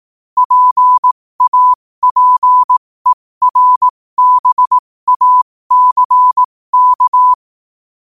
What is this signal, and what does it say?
Morse code, heard as a keyed tone, about 18 wpm, spaced as for 15 wpm, 1000 Hz. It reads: PAPERBACK
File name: qotdfiles / 15wpm